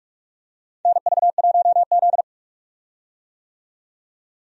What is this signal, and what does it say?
Morse code, heard as a keyed tone, about 46 wpm, 700 Hz. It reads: NV1Z
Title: USA Call Signs - 4 Characters - 45wpm